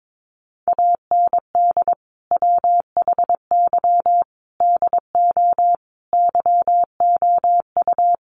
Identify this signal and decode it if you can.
Morse code, heard as a keyed tone, about 22 wpm, 700 Hz. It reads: AND WHY DO YOU
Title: Sentences from Top 400 Words 22wpm